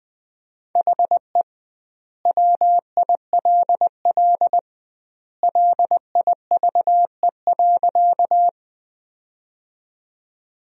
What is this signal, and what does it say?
Morse code, heard as a keyed tone, about 21 wpm, 700 Hz. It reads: HE WILL LIVE.
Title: 2x Word-Spacing - Sentences from Top 200 Words - 20wpm